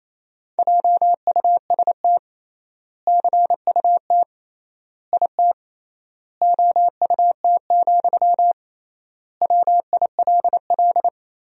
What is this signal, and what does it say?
Morse code, heard as a keyed tone, about 28 wpm, 700 Hz. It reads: JUST CUT IT OUT, WILL
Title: 3x Word-Spacing - Sentences from Top 300 Words - 28wpm